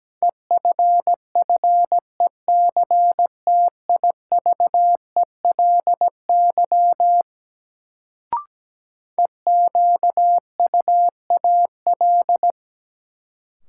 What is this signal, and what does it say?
Morse code, heard as a keyed tone, about 17 wpm, 700 Hz. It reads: EFFECTIVELY EQUAL
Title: Sentences from Top 1400 Words - New Word Drill - 17wpm